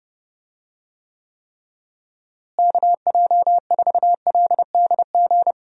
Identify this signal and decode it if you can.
Morse code, heard as a keyed tone, about 30 wpm, 700 Hz. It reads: KJ4LDG
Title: USA Call Signs - 6 Characters - Repeated 3x - 30wpm